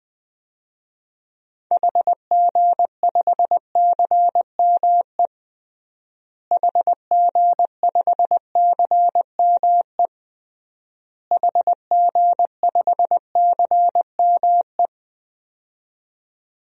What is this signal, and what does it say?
Morse code, heard as a keyed tone, about 20 wpm, 700 Hz. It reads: HG5CME HG5CME HG5CME
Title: DXCC-100 Call Signs - 6 Characters - Repeated 3x - 20wpm